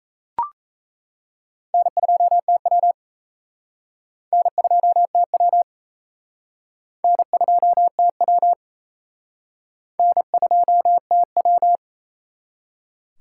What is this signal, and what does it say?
Morse code, heard as a keyed tone, about 36 wpm, 700 Hz. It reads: N2TW N2TW N2TW N2TW